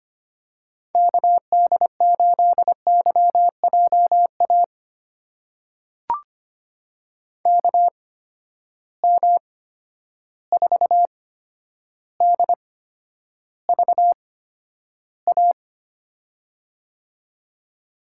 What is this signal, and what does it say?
Morse code, heard as a keyed tone, about 25 wpm, spaced as for 11 wpm, 700 Hz. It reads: KD8YJA KM4DVA
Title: USA Call Signs - 6 Characters - 8x Character Spacing - 25wpm